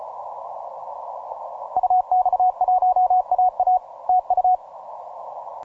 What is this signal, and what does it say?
Morse code, heard as a keyed tone, about 36 wpm, 740 Hz. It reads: UX1AA TU
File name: simple-file-list / MM0UKI EU189 ROCKALL 20MCW